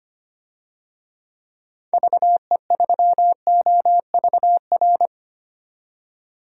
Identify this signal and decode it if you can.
Morse code, heard as a keyed tone, about 25 wpm, 700 Hz. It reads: VE3OVR
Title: DXCC-100 Call Signs - 6 Characters - 25wpm